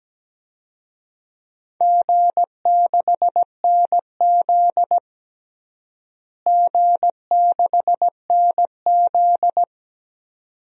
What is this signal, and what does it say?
Morse code, heard as a keyed tone, about 17 wpm, 700 Hz. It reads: G6NZ G6NZ
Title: DXCC-100 Call Signs - 4 Characters - Repeated 2x - 17wpm